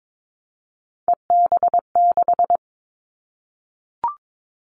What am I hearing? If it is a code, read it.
Morse code, heard as a keyed tone, about 22 wpm, 700 Hz. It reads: EB6
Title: Callsign Prefixes for Top 50 DXCC - 22wpm